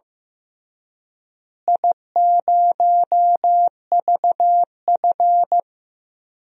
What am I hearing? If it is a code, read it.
Morse code, heard as a keyed tone, about 15 wpm, 700 Hz. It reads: I0VF
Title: DXCC-100 Call Signs - 4 Characters - Repeated 3x - 15wpm